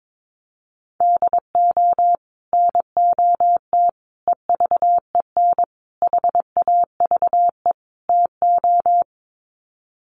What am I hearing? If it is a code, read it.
Morse code, heard as a keyed tone, about 22 wpm, 700 Hz. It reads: DO NOT EVEN HAVE TO
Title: Sets of 5 Words Encore2 22wpm